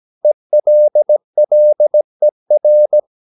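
Morse code, heard as a keyed tone, about 17 wpm, 600 Hz. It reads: ELLER
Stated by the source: Slow (QRS) Morse practice at 17 wpm, from CW med Gnister.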